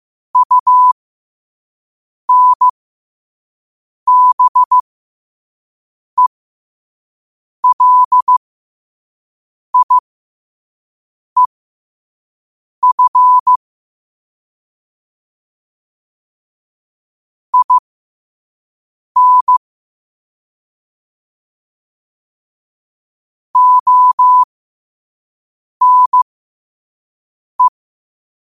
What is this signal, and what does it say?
Morse code, heard as a keyed tone, about 15 wpm, spaced as for 5 wpm, 1000 Hz. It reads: UNBELIEF IN ONE
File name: qotdfiles / 5wpm